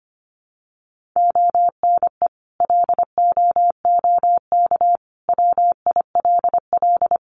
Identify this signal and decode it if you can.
Morse code, heard as a keyed tone, about 25 wpm, 700 Hz. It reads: ONE LOOK WILL
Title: Sentences from Top 1400 Words 25wpm